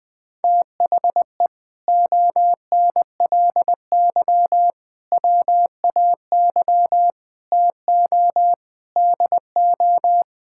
Morse code, heard as a keyed tone, about 20 wpm, 700 Hz. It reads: THE ONLY WAY TO DO
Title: Sentences with 8-Word Phrases - Spoken Beginning - 20wpm